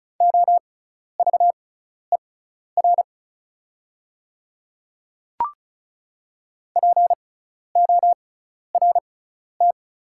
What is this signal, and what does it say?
Morse code, heard as a keyed tone, about 36 wpm, spaced as for 12 wpm, 690 Hz. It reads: OVER PORT
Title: 6x Character Spacing - Top 200 Words - 35wpm